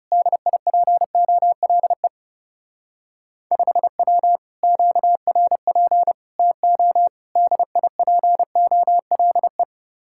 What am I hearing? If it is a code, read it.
Morse code, heard as a keyed tone, about 36 wpm, 700 Hz. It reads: DIPOLE 5W QRP TO DIPOLE
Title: Speed-Racing Common QSO Phrases - 30wpm